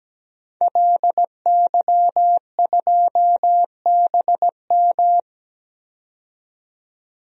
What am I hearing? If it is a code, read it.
Morse code, heard as a keyed tone, about 17 wpm, 700 Hz. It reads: LY2BM
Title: DXCC-100 Call Signs - 5 Characters - 17wpm